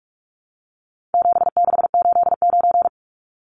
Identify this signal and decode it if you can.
Morse code, heard as a keyed tone, about 44 wpm, 700 Hz. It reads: 7689
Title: Numbers - 4-Digit 45wpm